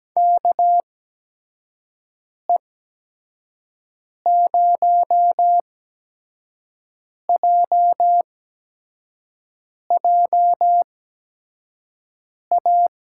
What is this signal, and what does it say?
Morse code, heard as a keyed tone, about 17 wpm, spaced as for 5 wpm, 700 Hz. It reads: KE0JJA
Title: USA Call Signs - 6 Characters - 8x Character Spacing - 17wpm